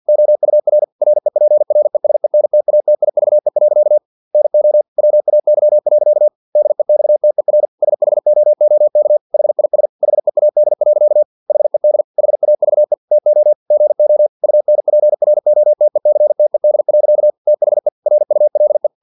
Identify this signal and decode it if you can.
Morse code, heard as a keyed tone, about 50 wpm, 600 Hz. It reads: OUR REPRESENTATIVE. NO WAY. DEXTER SHOOK HIS HEAD. HED HAVE TO GO UNPROTECTED. THE RULE
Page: Never 49wpm